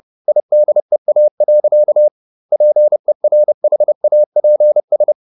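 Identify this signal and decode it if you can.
Morse code, heard as a keyed tone, about 30 wpm, 600 Hz. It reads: IDEA. PERHAPS
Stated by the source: Morse code at 30 wpm, the novel Never by Ken Follett in English, from CW med Gnister.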